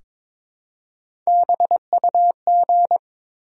Morse code, heard as a keyed tone, about 22 wpm, 700 Hz. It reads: BUG
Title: Three Letter Words - 22wpm